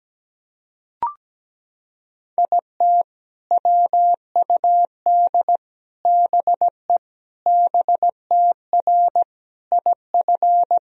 Morse code, heard as a keyed tone, about 17 wpm, 700 Hz. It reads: IT WUD BE BTR IF
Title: Sentences from Top 400 Words and Abbreviations 17wpm